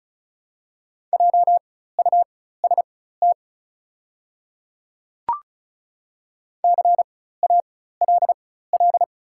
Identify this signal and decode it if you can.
Morse code, heard as a keyed tone, about 36 wpm, spaced as for 16 wpm, 690 Hz. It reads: JUST CALL
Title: 4x Character Spacing - Top 300 Words - 35wpm